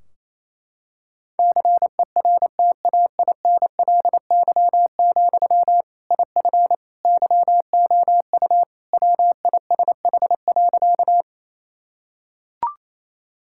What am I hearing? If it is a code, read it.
Morse code, heard as a keyed tone, about 28 wpm, 700 Hz. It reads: CERTAINLY, IF YOU WISH.
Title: Sentences from Top 1600 Words 28wpm